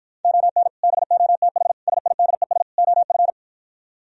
Morse code, heard as a keyed tone, about 52 wpm, 700 Hz. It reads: ON BOTH SIDES OF